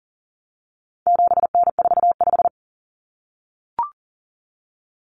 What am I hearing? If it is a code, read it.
Morse code, heard as a keyed tone, about 40 wpm, 700 Hz. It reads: 7N45
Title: Common Cut Numbers - 4-Digit 40wpm